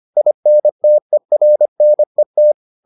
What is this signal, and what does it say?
Morse code, heard as a keyed tone, about 25 wpm, 600 Hz. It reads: INTERNET